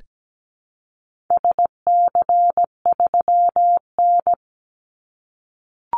Morse code, heard as a keyed tone, about 17 wpm, 700 Hz. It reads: SC3N
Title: Call Signs - 17wpm